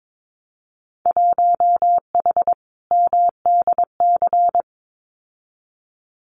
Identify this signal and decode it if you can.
Morse code, heard as a keyed tone, about 22 wpm, 700 Hz. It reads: 1H MDC
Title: Winter Field Day Exchanges 22wpm